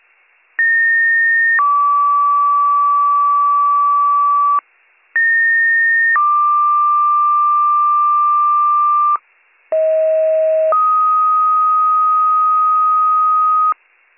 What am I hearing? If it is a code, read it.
Two-tone page, heard as a one-second tone, then a three-second tone, about 1850 then 1150 Hz, about 1850 then 1180 Hz, about 640 then 1250 Hz. It is heard from 0.6 to 13.7 s.